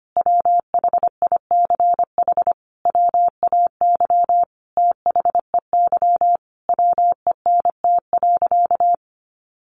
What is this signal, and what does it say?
Morse code, heard as a keyed tone, about 25 wpm, 700 Hz. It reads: WHICH WAY THEY WENT.
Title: Sentences from Top 1300 Words 25wpm